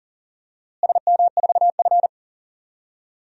Morse code, heard as a keyed tone, about 40 wpm, 700 Hz. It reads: SM4F
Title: DXCC-100 Call Signs - 4 Characters - Repeated 2x - 40wpm